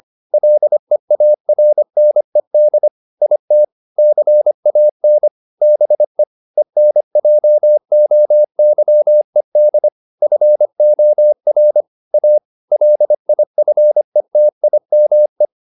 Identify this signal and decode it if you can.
Morse code, heard as a keyed tone, about 25 wpm, 600 Hz. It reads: LEARNED IT CAN BE ENJOYED FOR A LIFETIME